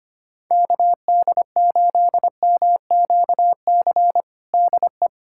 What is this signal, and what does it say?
Morse code, heard as a keyed tone, about 25 wpm, 700 Hz. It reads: KD8MQC DE
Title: USA Callsign DE Callsign 25wpm